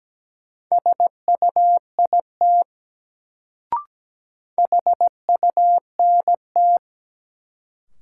Morse code, heard as a keyed tone, about 17 wpm, 700 Hz. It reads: SUIT HUNT
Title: Sentences from Top 1600 Words - New Word Drill Alt - 17wpm